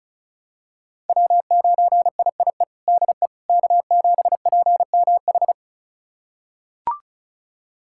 Morse code, heard as a keyed tone, about 36 wpm, 690 Hz. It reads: W9IIE DE K7PMH
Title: USA Callsign DE Callsign - 5 Characters - 35wpm